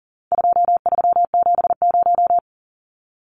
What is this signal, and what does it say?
Morse code, heard as a keyed tone, about 40 wpm, 700 Hz. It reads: 2370
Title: Numbers - 4-Digit 40wpm